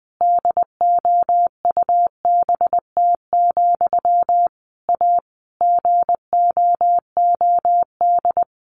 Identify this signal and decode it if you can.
Morse code, heard as a keyed tone, about 19 wpm, 700 Hz. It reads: DOUBT, A GOOD
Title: Sentences from Top 1500 Words 20wpm